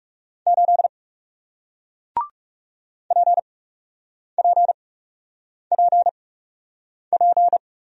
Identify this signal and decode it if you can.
Morse code, heard as a keyed tone, about 41 wpm, spaced as for 8 wpm, 700 Hz. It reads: 8 PPPP